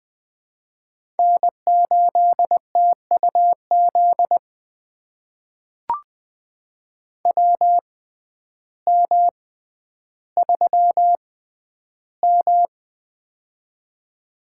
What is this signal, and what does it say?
Morse code, heard as a keyed tone, about 20 wpm, spaced as for 11 wpm, 700 Hz. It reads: N8TUZ WM3M
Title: USA Call Signs - 6x Character Spacing - 20wpm